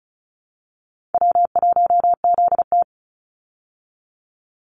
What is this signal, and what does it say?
Morse code, heard as a keyed tone, about 34 wpm, 700 Hz. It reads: W1ZT
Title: Call Signs Repeated 2x - 35wpm